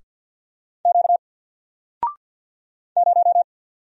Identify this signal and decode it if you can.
Morse code, heard as a keyed tone, about 49 wpm, 700 Hz. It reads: Q 0
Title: Single Letter-Number Alt 50wpm